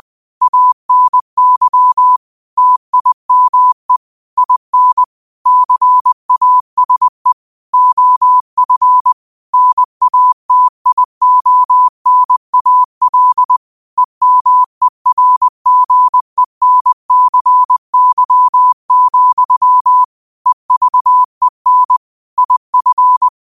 Morse code, heard as a keyed tone, about 20 wpm, 1000 Hz. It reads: ANY TIME IN CASE OF NATIONAL EMERGENCY, EVEN IF
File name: qotdfiles / 20wpm